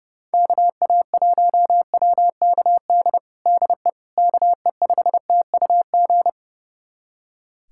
Morse code, heard as a keyed tone, about 30 wpm, 700 Hz. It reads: KA1WKD DE KE5TUG
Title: USA Callsign DE Callsign - 6 Characters - 30wpm